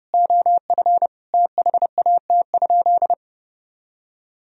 Morse code, heard as a keyed tone, about 30 wpm, 700 Hz. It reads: OF THAT?
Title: Sentences from Top 1000 Words - Review Part 3 - 30wpm